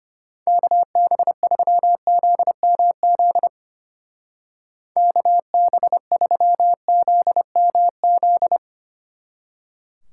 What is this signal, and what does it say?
Morse code, heard as a keyed tone, about 33 wpm, 700 Hz. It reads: KB3ZMZ KB3ZMZ